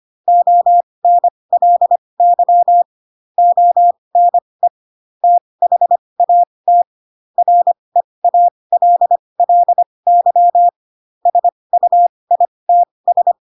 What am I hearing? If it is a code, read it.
Morse code, heard as a keyed tone, about 25 wpm, spaced as for 20 wpm, 700 Hz. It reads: ONLY ONE THAT REALLY SUITS
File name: Pinocchio / Pinocchio ch4 15 25 ik1hge